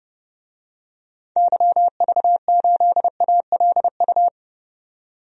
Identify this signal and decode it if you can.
Morse code, heard as a keyed tone, about 31 wpm, 700 Hz. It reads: YV8ALU
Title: DXCC-100 Call Signs - 6 Characters - 30wpm